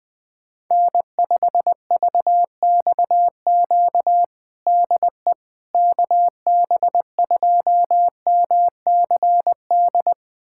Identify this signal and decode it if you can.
Morse code, heard as a keyed tone, about 20 wpm, 700 Hz. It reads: N5VXQ DE KB2MCD
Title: USA Callsign DE Callsign 20wpm